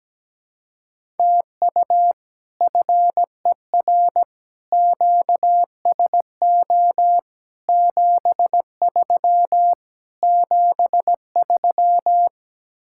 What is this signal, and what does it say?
Morse code, heard as a keyed tone, about 17 wpm, 700 Hz. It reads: TU FER QSO 73 73
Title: Speed-Racing Common QSO Phrases - 17wpm